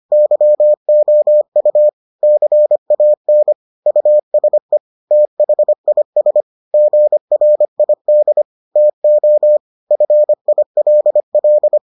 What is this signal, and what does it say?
Morse code, heard as a keyed tone, about 25 wpm, 600 Hz. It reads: YOU CAN USE THIS GRID TO FILL